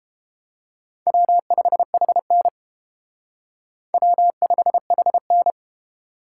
Morse code, heard as a keyed tone, about 29 wpm, 700 Hz. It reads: W5HN W5HN